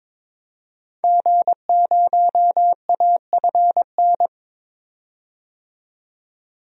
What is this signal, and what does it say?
Morse code, heard as a keyed tone, about 22 wpm, 700 Hz. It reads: G0AFN
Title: DXCC-100 Call Signs - 5 Characters - 22wpm